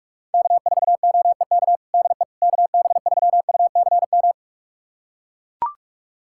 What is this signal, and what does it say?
Morse code, heard as a keyed tone, about 43 wpm, 700 Hz. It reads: K4OEK DE KB3UCM
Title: USA Callsign DE Callsign 45wpm